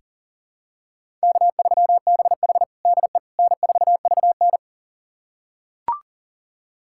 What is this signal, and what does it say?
Morse code, heard as a keyed tone, about 37 wpm, 700 Hz. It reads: K3BH DE N4VN
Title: USA Callsign DE Callsign - 4 Characters - 40wpm